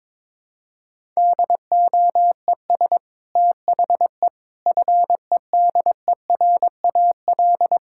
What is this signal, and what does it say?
Morse code, heard as a keyed tone, about 21 wpm, 700 Hz. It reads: DOES THE FEDERAL
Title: Sentences from Top 1000 Words - Review Part 5 - 22wpm